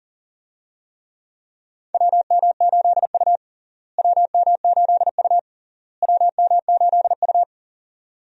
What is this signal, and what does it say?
Morse code, heard as a keyed tone, about 37 wpm, 700 Hz. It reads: WM8U WM8U WM8U
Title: USA Call Signs - 4 Characters - Repeated 3x - 40wpm